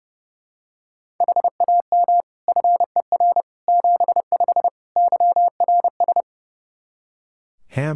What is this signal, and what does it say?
Morse code, heard as a keyed tone, about 30 wpm, 700 Hz. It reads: HAM FER 75 YRS
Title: Practice QSO - Exchange 3 - More Info - 30wpm